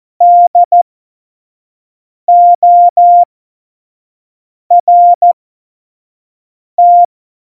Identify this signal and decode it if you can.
Morse code, heard as a keyed tone, about 14 wpm, spaced as for 5 wpm, 700 Hz. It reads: DORT
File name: Morsekurs / Witze / Witz 08. Mai 2018, 25 F70